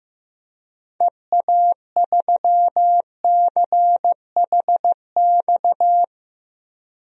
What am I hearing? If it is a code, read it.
Morse code, heard as a keyed tone, about 15 wpm, 700 Hz. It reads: EA3CHX
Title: DXCC-100 Call Signs - 6 Characters - Repeated 2x - 15wpm